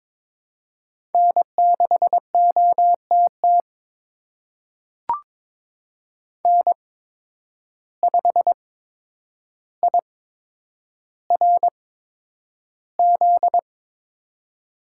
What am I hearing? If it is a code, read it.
Morse code, heard as a keyed tone, about 22 wpm, spaced as for 9 wpm, 700 Hz. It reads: N6OTT N5IRZ
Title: USA Call Signs - 5 Characters - 8x Character Spacing - 22wpm